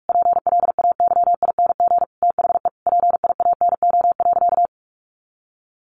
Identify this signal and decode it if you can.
Morse code, heard as a keyed tone, about 44 wpm, 700 Hz. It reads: PLAYING THE PIANO.
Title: Additional Easy Phrases 45wpm